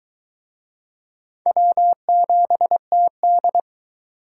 Morse code, heard as a keyed tone, about 23 wpm, 700 Hz. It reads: W7TD